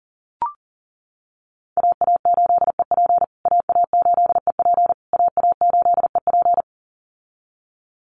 Morse code, heard as a keyed tone, about 40 wpm, 700 Hz. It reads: AA8EP AA8EP AA8EP
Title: USA Call Signs Repeated 3x 40wpm